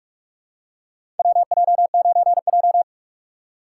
Morse code, heard as a keyed tone, about 42 wpm, 700 Hz. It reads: WJ9J
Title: USA Call Signs - 4 Characters - 2x Character Spacing - 45wpm